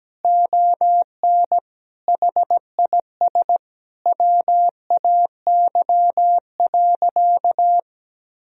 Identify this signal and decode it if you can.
Morse code, heard as a keyed tone, about 17 wpm, 700 Hz. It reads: ON HIS WAY.
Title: Sentences from Top 300 Words and Abbreviations 17wpm